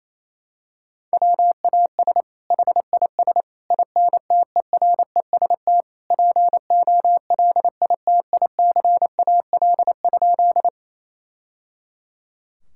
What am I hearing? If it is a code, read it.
Morse code, heard as a keyed tone, about 28 wpm, 700 Hz. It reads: WAS HIS INTEREST POLITICAL?
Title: Sentences from Top 1000 Words - Review Part 5 - 28wpm